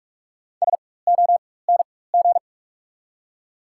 Morse code, heard as a keyed tone, about 43 wpm, spaced as for 21 wpm, 700 Hz. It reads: SONG
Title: 4x Character Spacing - Top 500 Words - 45wpm